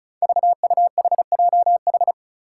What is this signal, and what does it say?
Morse code, heard as a keyed tone, about 35 wpm, 700 Hz. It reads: VUHJH
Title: Five Challenging Letter-Number Combinations 35wpm